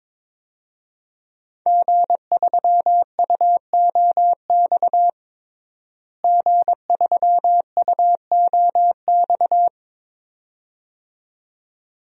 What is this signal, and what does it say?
Morse code, heard as a keyed tone, about 22 wpm, 700 Hz. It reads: G3UOX G3UOX
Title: DXCC-100 Call Signs - 5 Characters - Repeated 2x - 22wpm